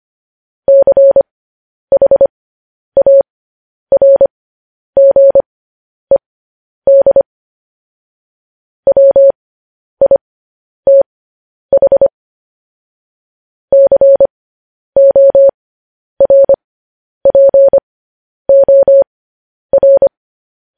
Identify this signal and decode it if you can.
Morse code, heard as a keyed tone, about 25 wpm, spaced as for 10 wpm, 550 Hz. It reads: CHARGED WITH CORPOR
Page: News Headlines in Morse Code at 10 WPM